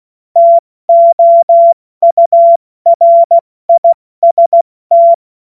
Morse code, heard as a keyed tone, about 16 wpm, 680 Hz. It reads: TOURIST
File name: Morsekurs / Witze / Witz 21. Februar 2023 70 F80